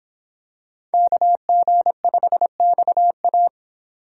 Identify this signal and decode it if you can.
Morse code, heard as a keyed tone, about 24 wpm, 700 Hz. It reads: KG5XA